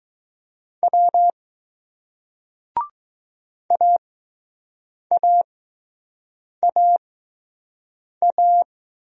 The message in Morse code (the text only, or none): W AAAA